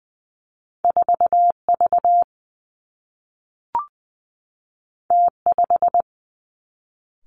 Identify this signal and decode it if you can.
Morse code, heard as a keyed tone, about 20 wpm, 700 Hz. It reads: 4 V T 5
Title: Two Challenging Letter-Number Combinations 20wpm